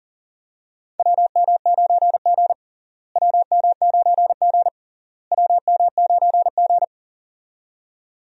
Morse code, heard as a keyed tone, about 40 wpm, 700 Hz. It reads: WM9G WM9G WM9G
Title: USA Call Signs - 4 Characters - Repeated 3x - 40wpm